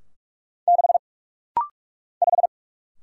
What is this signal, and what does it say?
Morse code, heard as a keyed tone, about 47 wpm, 700 Hz. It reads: 6 5
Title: Single Letter-Number - Rapid-Fire 45wpm